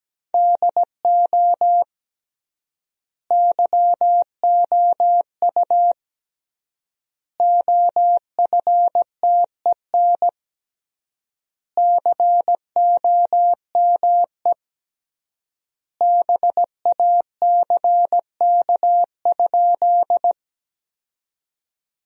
Morse code, heard as a keyed tone, about 17 wpm, 700 Hz. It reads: DO YOU OFTEN COME BACK?
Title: 3x Word-Spacing - Sentences from Top 300 Words - 17wpm